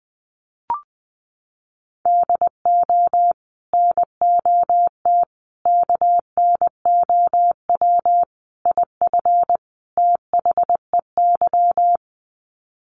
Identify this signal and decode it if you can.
Morse code, heard as a keyed tone, about 19 wpm, 700 Hz. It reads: DO NOT KNOW IF THEY
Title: Sets of 5 Words Encore 20wpm